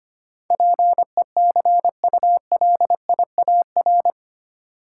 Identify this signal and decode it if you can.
Morse code, heard as a keyed tone, about 25 wpm, 690 Hz. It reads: PECULIAR
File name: 8 Letter Words - First 3 Letters Given - 25wpm